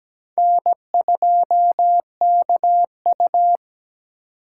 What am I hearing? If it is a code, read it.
Morse code, heard as a keyed tone, about 20 wpm, 700 Hz. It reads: N2KU
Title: Speed Racing - USA 1x2 Call Signs 17wpm